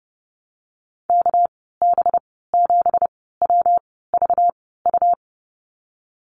Morse code, heard as a keyed tone, about 30 wpm, spaced as for 17 wpm, 700 Hz. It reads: KB7WVU
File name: Lesson 37g - Letter x - Call Signs 30wpm